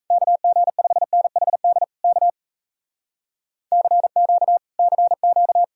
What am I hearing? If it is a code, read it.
Morse code, heard as a keyed tone, about 37 wpm, 700 Hz. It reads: KG5NHD K CQ CQ